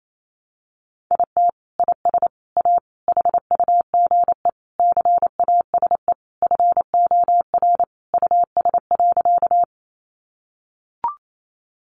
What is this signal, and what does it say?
Morse code, heard as a keyed tone, about 28 wpm, 700 Hz. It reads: IT IS A HUGE CASE FOR US.
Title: Sentences from Top 1100 Words 28wpm